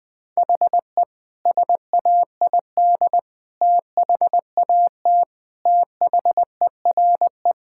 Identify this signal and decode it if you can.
Morse code, heard as a keyed tone, about 21 wpm, 700 Hz. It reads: HE SAID THAT THERE
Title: Sentences with 3-Word Phrases - 20wpm